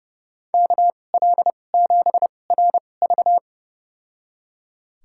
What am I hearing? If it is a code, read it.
Morse code, heard as a keyed tone, about 30 wpm, spaced as for 22 wpm, 690 Hz. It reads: KL7RV
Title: USA Call Signs - 5 Characters - 2x Character Spacing - 30wpm